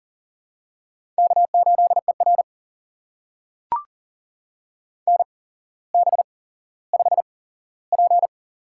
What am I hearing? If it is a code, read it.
Morse code, heard as a keyed tone, about 39 wpm, spaced as for 17 wpm, 700 Hz. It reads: K8ER NB5P
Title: USA Call Signs - 4 Characters - 8x Character Spacing - 40wpm